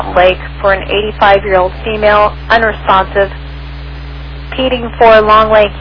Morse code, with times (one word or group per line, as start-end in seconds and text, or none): none